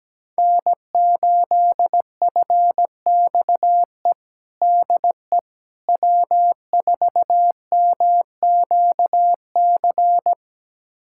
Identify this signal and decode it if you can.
Morse code, heard as a keyed tone, about 17 wpm, 700 Hz. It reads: N8FXE DE W4MQC
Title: USA Callsign DE Callsign 17wpm